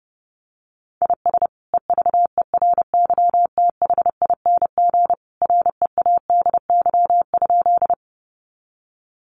Morse code, heard as a keyed tone, about 30 wpm, 700 Hz. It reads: IS EVERYTHING READY?
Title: Sentences from Top 700 Words 30wpm